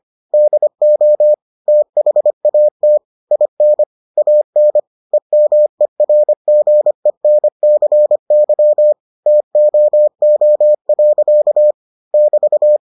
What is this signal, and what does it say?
Morse code, heard as a keyed tone, about 25 wpm, 600 Hz. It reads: DO THAT IN AN EMERGENCY TOO. <BT>